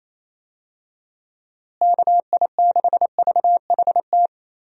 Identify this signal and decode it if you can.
Morse code, heard as a keyed tone, about 27 wpm, 700 Hz. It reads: KI6VHT
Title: USA Call Signs - 6 Characters - Repeated 3x - 28wpm